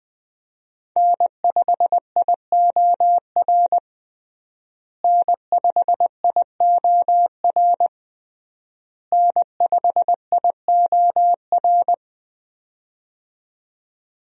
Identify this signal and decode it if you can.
Morse code, heard as a keyed tone, about 21 wpm, 700 Hz. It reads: N5IOR N5IOR N5IOR
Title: USA Call Signs - 5 Characters - Repeated 3x - 20wpm